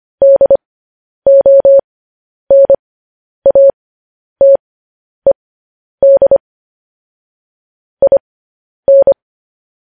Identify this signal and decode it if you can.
Morse code, heard as a keyed tone, about 25 wpm, spaced as for 10 wpm, 550 Hz. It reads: DONATED IN
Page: news-headlines-in-morse-code-at-10-wpm